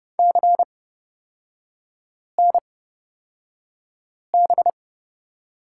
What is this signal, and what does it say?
Morse code, heard as a keyed tone, about 30 wpm, spaced as for 5 wpm, 700 Hz. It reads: CNB
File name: Lesson 36e - BT - Three Characters 30wpm